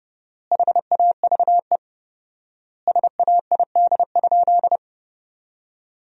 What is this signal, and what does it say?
Morse code, heard as a keyed tone, about 32 wpm, 700 Hz. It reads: HAVE SAID?
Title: 4x Word-Spacing - Sentences from Top 100 Words - 30wpm